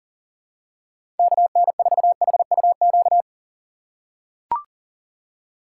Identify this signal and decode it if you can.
Morse code, heard as a keyed tone, about 37 wpm, 700 Hz. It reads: KN4HUQ
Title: USA Call Signs - 6x Character Spacing - 40wpm